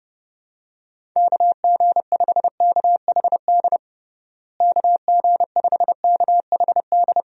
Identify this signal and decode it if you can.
Morse code, heard as a keyed tone, about 31 wpm, 700 Hz. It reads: KG5KHD KG5KHD
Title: USA Call Signs - 6 Characters - Repeated 3x - 30wpm